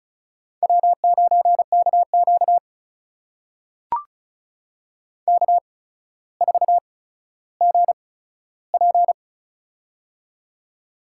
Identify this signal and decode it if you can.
Morse code, heard as a keyed tone, about 36 wpm, spaced as for 15 wpm, 700 Hz. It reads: W9KQ K4GP
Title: USA Call Signs - 4 Characters - 8x Character Spacing - 35wpm